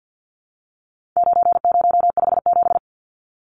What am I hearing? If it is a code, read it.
Morse code, heard as a keyed tone, about 50 wpm, 700 Hz. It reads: 9057
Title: Numbers - 4-Digit 50wpm